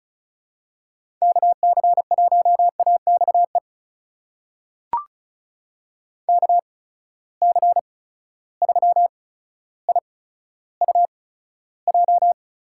Morse code, heard as a keyed tone, about 36 wpm, spaced as for 15 wpm, 700 Hz. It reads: KC1AXE KC3IUJ
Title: USA Call Signs - 6 Characters - 8x Character Spacing - 35wpm